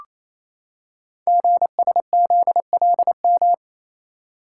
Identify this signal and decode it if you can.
Morse code, heard as a keyed tone, about 29 wpm, 700 Hz. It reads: GSZLM
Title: Code Groups - 5 Letters - 28wpm